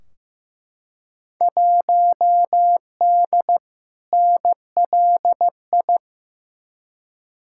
Morse code, heard as a keyed tone, about 15 wpm, 700 Hz. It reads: 1D NLI
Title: Field Day Exchanges - 15wpm